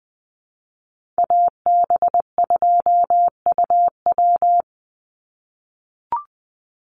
Morse code, heard as a keyed tone, about 20 wpm, 700 Hz. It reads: AB2UW
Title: USA Call Signs 20wpm